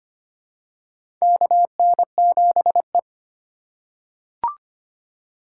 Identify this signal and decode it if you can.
Morse code, heard as a keyed tone, about 25 wpm, 700 Hz. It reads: KN7E
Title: USA Call Signs - 4 Characters - 2x Character Spacing - 25wpm